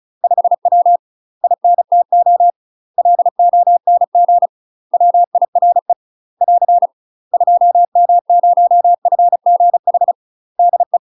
Morse code, heard as a keyed tone, about 36 wpm, 700 Hz. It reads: 5W INTO LONG WIRE <AR> 2M0FGH DE
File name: UK MASTER QSOs TRAINING / 35WPM / UK MASTER QSOs 0024